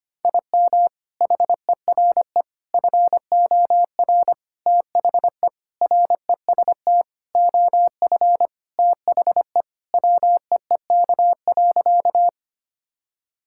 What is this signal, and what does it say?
Morse code, heard as a keyed tone, about 25 wpm, 700 Hz. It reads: IM HERE FOR THE REST OF THE WEEK.
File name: practice / 25wpm / Sentences with 4-Word Phrases - 25wpm